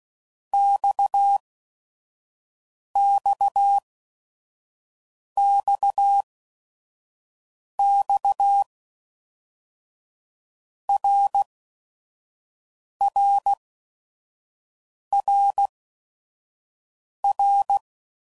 Morse code, heard as a keyed tone, about 16 wpm, 800 Hz. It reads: X X X X R R R R
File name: Basic Letter Groups By7Groups / RKPX / RKPX@5 f15